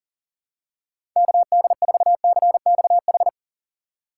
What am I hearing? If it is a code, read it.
Morse code, heard as a keyed tone, about 37 wpm, 700 Hz. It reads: KD4CXH
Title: USA Call Signs - 6 Characters - 6x Character Spacing - 40wpm